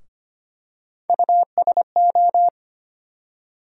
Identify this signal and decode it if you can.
Morse code, heard as a keyed tone, about 25 wpm, 700 Hz. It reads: USO
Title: USA Call Sign Postfixes - 3 Letters 25wpm